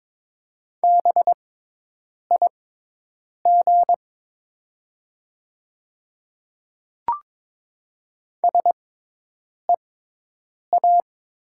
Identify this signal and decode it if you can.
Morse code, heard as a keyed tone, about 22 wpm, spaced as for 8 wpm, 700 Hz. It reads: BIG SEA
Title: 6x Character Spacing - Top 300 Words - 22wpm